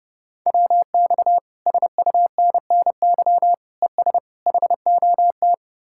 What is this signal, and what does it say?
Morse code, heard as a keyed tone, about 40 wpm, 700 Hz. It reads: WX SUNNY ES HOT